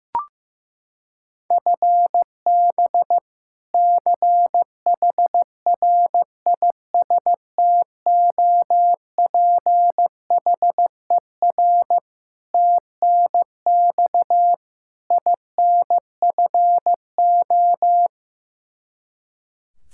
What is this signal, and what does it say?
Morse code, heard as a keyed tone, about 15 wpm, 700 Hz. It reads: FB CHRISTOPHER TNX INFO
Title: K1USN SST QSO Practice 15wpm